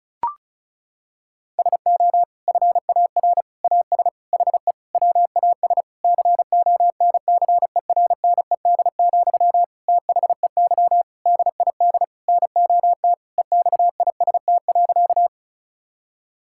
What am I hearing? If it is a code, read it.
Morse code, heard as a keyed tone, about 36 wpm, 700 Hz. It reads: SO FAR AS HE WAS CONCERNED, THEY DID NOT EXIST.
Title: Sentences from Top 1000 Words - Review Part 6 - 35wpm